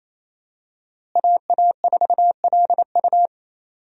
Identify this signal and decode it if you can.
Morse code, heard as a keyed tone, about 28 wpm, 700 Hz. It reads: AA4LU
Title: USA Call Signs - 5 Characters - Repeated 3x - 28wpm